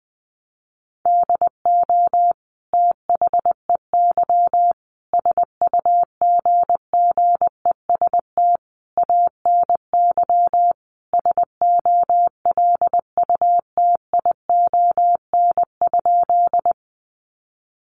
Morse code, heard as a keyed tone, about 19 wpm, 700 Hz. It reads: DO THEY SUGGEST ANY SOLUTION?
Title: Sentences from Top 1300 Words 20wpm